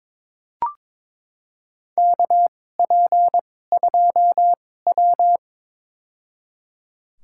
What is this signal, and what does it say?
Morse code, heard as a keyed tone, about 22 wpm, spaced as for 16 wpm, 700 Hz. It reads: KP2W
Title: USA Call Signs - 2x Character Spacing - 22wpm